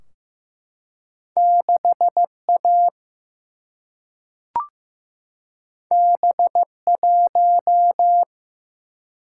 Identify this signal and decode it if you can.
Morse code, heard as a keyed tone, about 16 wpm, 700 Hz. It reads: 6 A B 1
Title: Single-Letter Single-Number 15wpm